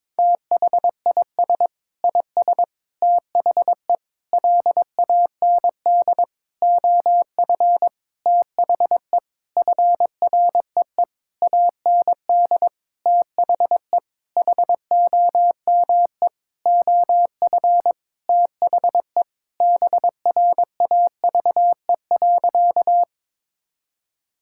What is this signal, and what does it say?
Morse code, heard as a keyed tone, about 22 wpm, 700 Hz. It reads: THIS IS THE LAND OF THE FREE AND THE HOME OF THE BRAVE.
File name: Sentences with 9-Word Phrases - 22wpm